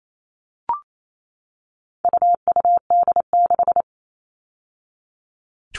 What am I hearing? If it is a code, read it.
Morse code, heard as a keyed tone, about 28 wpm, 700 Hz. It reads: UUD6
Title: All Cut Numbers - 4-Digit 28wpm